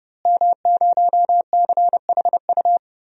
Morse code, heard as a keyed tone, about 30 wpm, 700 Hz. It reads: M0CHU
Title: DXCC-100 Call Signs - 5 Characters - Repeated 2x - 30wpm